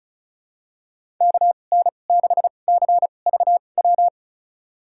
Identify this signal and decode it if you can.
Morse code, heard as a keyed tone, about 34 wpm, spaced as for 25 wpm, 690 Hz. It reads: KN6CVW
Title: USA Call Signs - 6 Characters - 2x Character Spacing - 35wpm